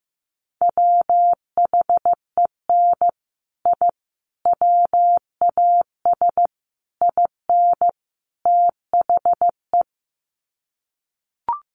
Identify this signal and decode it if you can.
Morse code, heard as a keyed tone, about 15 wpm, 700 Hz. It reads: WHEN I WAS IN THE
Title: Sets of 5 Words Encore2 15wpm